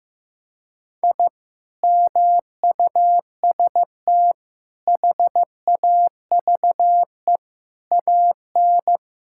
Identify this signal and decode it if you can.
Morse code, heard as a keyed tone, about 15 wpm, 700 Hz. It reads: I MUST HAVE AN
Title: Sentences from Top 1000 Words - Review Part 5 - 15wpm